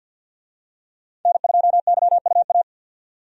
Morse code, heard as a keyed tone, about 50 wpm, 700 Hz. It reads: N2YUA
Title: USA Call Signs - 5 Characters - 50wpm